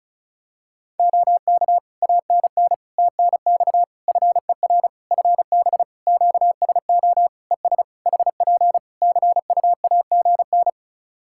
Morse code, heard as a keyed tone, about 34 wpm, 700 Hz. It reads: OK ANN TNX FER FB QSO ES HP CUAGN
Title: Practice QSO - Exchange 4 - Ending - 35wpm